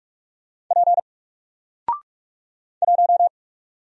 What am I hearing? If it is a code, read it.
Morse code, heard as a keyed tone, about 44 wpm, 700 Hz. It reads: P 1
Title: Single Character (Letter:Number:Punctuation:Prosign) Alt 45wpm